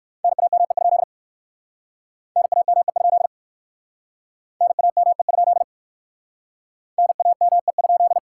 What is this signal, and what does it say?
Morse code, heard as a keyed tone, about 56 wpm, 700 Hz. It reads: NAME? NAME? NAME? NAME?
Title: Speed-Racing Common QSO Phrases Encore - 45wpm